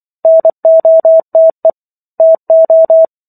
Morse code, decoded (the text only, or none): NOTE TO